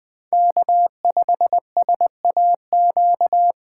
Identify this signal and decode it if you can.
Morse code, heard as a keyed tone, about 20 wpm, 700 Hz. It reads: K5SAQ